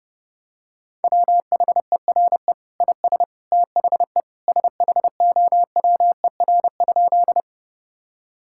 Morse code, heard as a keyed tone, about 32 wpm, 700 Hz. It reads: WHERE IS THE SHOWER?
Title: Sentences from Top 1600 Words 30wpm